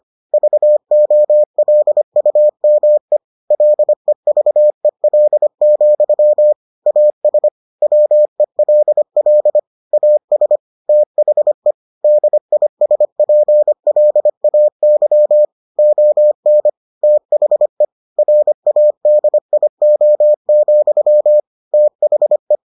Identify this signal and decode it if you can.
Morse code, heard as a keyed tone, about 25 wpm, 600 Hz. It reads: VOLUME LEVEL, AS WELL AS THE DISPLAY ON THE RADIO, THE